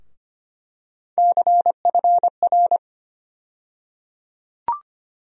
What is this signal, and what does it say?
Morse code, heard as a keyed tone, about 25 wpm, 700 Hz. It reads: CFR
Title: USA Call Sign Postfixes - 3 Letters 25wpm